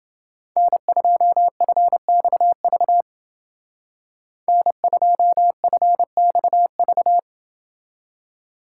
Morse code, heard as a keyed tone, about 29 wpm, 700 Hz. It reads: N2FXV N2FXV